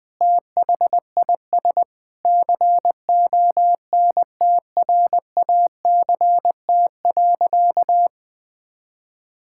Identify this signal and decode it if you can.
Morse code, heard as a keyed tone, about 20 wpm, 700 Hz. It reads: THIS CONTRACT.
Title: Sentences with 9-Word Phrases - 20wpm